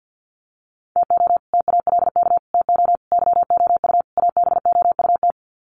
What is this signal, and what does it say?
Morse code, heard as a keyed tone, about 50 wpm, 700 Hz. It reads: TO TALK TO YOU ABOUT
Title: Sets of 5 Words Encore 50wpm